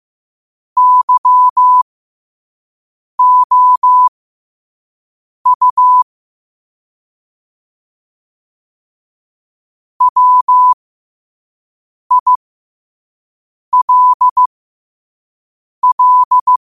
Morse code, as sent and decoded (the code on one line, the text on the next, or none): -.-- --- ..- / .-- .. .-.. .-..
YOU WILL